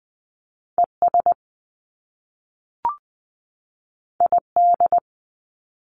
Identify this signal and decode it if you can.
Morse code, heard as a keyed tone, about 20 wpm, 700 Hz. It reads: ES ID
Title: Top Two Letter Combinations in Words 20wpm